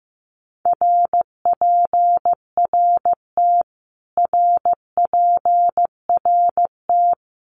Morse code, heard as a keyed tone, about 15 wpm, 700 Hz. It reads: RPRT RPRT
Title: Abbreviations Repeated 3x - 15wpm